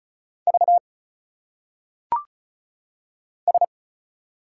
Morse code, heard as a keyed tone, about 34 wpm, 700 Hz. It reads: V S
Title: Single Letters - Phonetic Pronunciation 35wpm